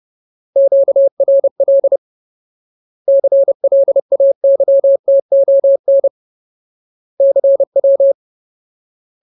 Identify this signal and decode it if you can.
Morse code, heard as a keyed tone, about 30 wpm, 550 Hz. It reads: QRL CLAYTON CW
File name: Ultimate Word List / 30WPM Ultimate